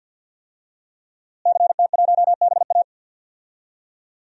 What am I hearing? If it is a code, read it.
Morse code, heard as a keyed tone, about 48 wpm, 700 Hz. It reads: CT1BA
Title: DXCC-100 Call Signs - 5 Characters - Repeated 3x - 50wpm